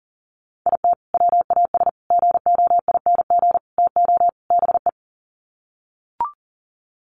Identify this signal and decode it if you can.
Morse code, heard as a keyed tone, about 40 wpm, 700 Hz. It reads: IT WAS GOING TO BE
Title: Sets of 5 Words 40wpm